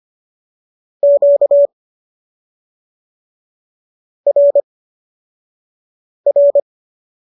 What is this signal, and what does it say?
Morse code, heard as a keyed tone, about 25 wpm, spaced as for 4 wpm, 570 Hz. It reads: QRR